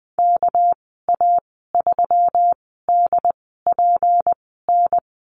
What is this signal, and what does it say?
Morse code, heard as a keyed tone, about 20 wpm, spaced as for 14 wpm, 700 Hz. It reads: KA3DPN
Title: 231 Lesson 37g - Letter x - Call Signs 20wpm